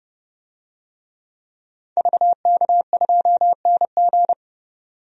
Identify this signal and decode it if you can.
Morse code, heard as a keyed tone, about 30 wpm, 700 Hz. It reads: VK2NG
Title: DXCC-100 Call Signs - 5 Characters - 30wpm